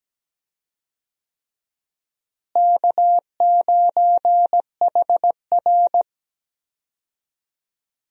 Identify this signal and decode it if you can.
Morse code, heard as a keyed tone, about 17 wpm, 700 Hz. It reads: K9HR
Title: USA Call Signs - 4 Characters - Repeated 3x - 17wpm